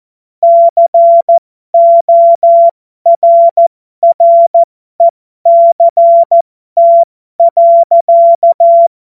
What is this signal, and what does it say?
Morse code, heard as a keyed tone, about 14 wpm, spaced as for 12 wpm, 680 Hz. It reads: CORRECT.